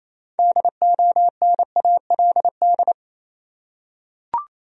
Morse code, heard as a keyed tone, about 28 wpm, 700 Hz. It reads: DONALD
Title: USA Common Names - 28wpm